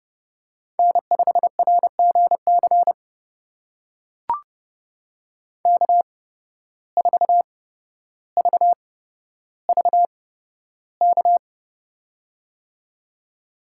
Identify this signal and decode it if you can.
Morse code, heard as a keyed tone, about 30 wpm, spaced as for 13 wpm, 700 Hz. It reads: N5RGC K4VVK
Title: USA Call Signs - 5 Characters - 8x Character Spacing - 30wpm